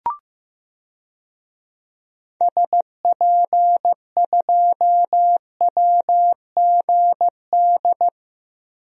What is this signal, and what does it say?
Morse code, heard as a keyed tone, about 16 wpm, 700 Hz. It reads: SP2WGD
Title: DXCC-100 Call Signs - 6 Characters - Repeated 2x - 15wpm